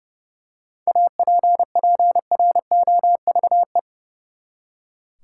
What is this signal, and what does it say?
Morse code, heard as a keyed tone, about 30 wpm, 700 Hz. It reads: APPROVE
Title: Seven Letter Words - 30wpm